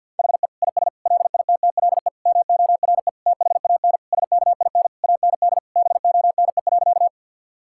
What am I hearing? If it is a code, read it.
Morse code, heard as a keyed tone, about 50 wpm, 700 Hz. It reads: HE IS LITTLE MORE THAN SKIN AND BONE.
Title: Sentences from Top 1600 Words 50wpm